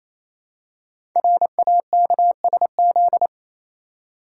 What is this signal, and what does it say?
Morse code, heard as a keyed tone, about 27 wpm, 700 Hz. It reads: RAKSZ
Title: Code Groups - 5 Letter:Numbers - 28wpm